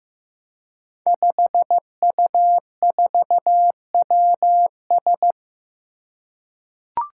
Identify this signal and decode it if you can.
Morse code, heard as a keyed tone, about 15 wpm, 700 Hz. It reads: 5U4WS
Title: Five Challenging Letter-Number Combinations 15wpm